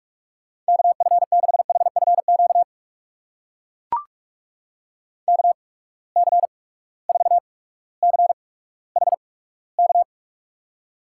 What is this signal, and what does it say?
Morse code, heard as a keyed tone, about 44 wpm, spaced as for 19 wpm, 700 Hz. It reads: KF6HFQ KC4CHK
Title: USA Call Signs - 6 Characters - 8x Character Spacing - 45wpm